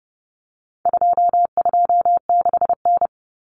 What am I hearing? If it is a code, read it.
Morse code, heard as a keyed tone, about 30 wpm, 700 Hz. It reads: 226N
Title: Common Cut Numbers - 4-Digit 30wpm